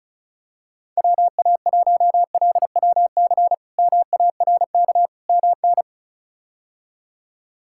Morse code, heard as a keyed tone, about 34 wpm, 700 Hz. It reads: WA1LWC MARK MN
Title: K1USN SST QSO Practice - 35wpm